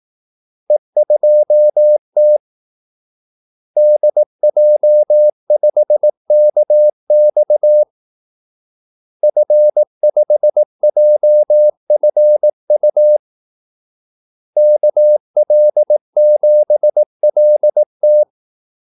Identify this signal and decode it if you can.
Morse code, heard as a keyed tone, about 18 wpm, 600 Hz. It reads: E2T DJ5KX F5JFU KL7LT
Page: Kaldesignaler 18 wpm